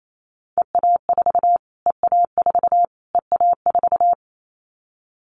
Morse code, heard as a keyed tone, about 28 wpm, 700 Hz. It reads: EA4 EA4 EA4
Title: Callsign Prefixes for Top 25 DXCC - Repeated 3x - 28wpm